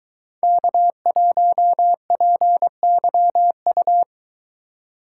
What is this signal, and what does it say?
Morse code, heard as a keyed tone, about 21 wpm, 700 Hz. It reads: K1PYU